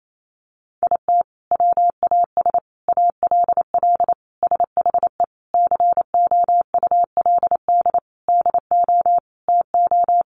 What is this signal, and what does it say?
Morse code, heard as a keyed tone, about 28 wpm, 700 Hz. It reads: IT WAS ALL SHE COULD DO TO
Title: Sentences from Top 400 Words 28wpm